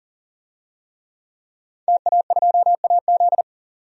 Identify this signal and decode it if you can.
Morse code, heard as a keyed tone, about 38 wpm, 700 Hz. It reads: TA2AZ
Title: DXCC-100 Call Signs - 5 Characters - 40wpm